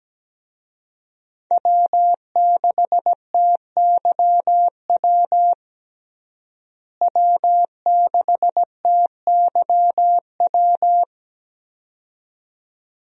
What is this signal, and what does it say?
Morse code, heard as a keyed tone, about 17 wpm, 700 Hz. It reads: W6TYW W6TYW
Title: USA Call Signs - 5 Characters - Repeated 2x - 17wpm